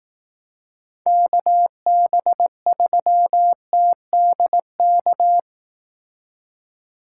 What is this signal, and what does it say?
Morse code, heard as a keyed tone, about 18 wpm, 700 Hz. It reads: KB3TDK
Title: Speed Racing - USA 2x3 Call Signs 15wpm